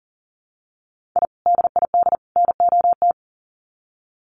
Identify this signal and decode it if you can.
Morse code, heard as a keyed tone, about 40 wpm, 700 Hz. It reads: I DID NOT
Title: Sets of 3 Words 40wpm